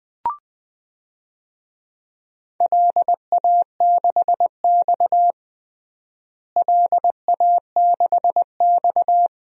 Morse code, heard as a keyed tone, about 21 wpm, 700 Hz. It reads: LA6X LA6X
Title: DXCC-100 Call Signs - 4 Characters - Repeated 2x - 20wpm